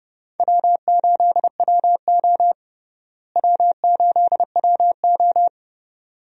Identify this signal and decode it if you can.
Morse code, heard as a keyed tone, about 30 wpm, 700 Hz. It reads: W8WO W8WO
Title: USA Call Signs - 4 Characters - Repeated 2x - 30wpm